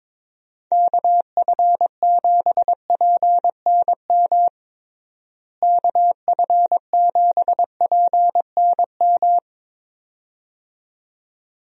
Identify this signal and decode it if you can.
Morse code, heard as a keyed tone, about 22 wpm, 700 Hz. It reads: KF7PNM KF7PNM
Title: USA Call Signs - 6 Characters - Repeated 2x - 22wpm